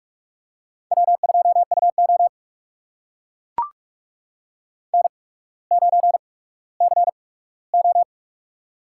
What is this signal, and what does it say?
Morse code, heard as a keyed tone, about 44 wpm, spaced as for 19 wpm, 690 Hz. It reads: W2UO N9CO
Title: USA Call Signs - 4 Characters - 8x Character Spacing - 45wpm